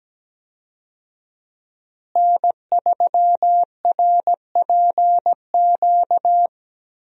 Morse code, heard as a keyed tone, about 17 wpm, 700 Hz. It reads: N3RPQ
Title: USA Call Signs - 5 Characters - Repeated 3x - 17wpm